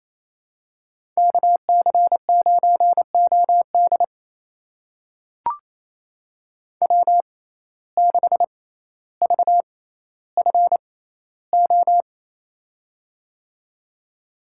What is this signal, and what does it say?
Morse code, heard as a keyed tone, about 28 wpm, spaced as for 14 wpm, 700 Hz. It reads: KC9OD W6VFO
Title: USA Call Signs - 5 Characters - 6x Character Spacing - 28wpm